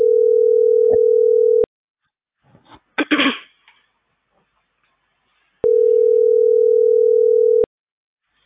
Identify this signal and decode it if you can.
Ringback tone (North America), 440 + 480 Hz.